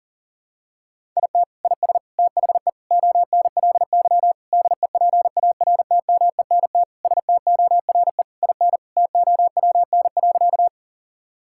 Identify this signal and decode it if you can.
Morse code, heard as a keyed tone, about 40 wpm, 700 Hz. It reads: IT IS THE ONLY DEPARTMENT STORE IN TOWN.
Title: Sentences from Top 1000 Words - Review Part 5 - 40wpm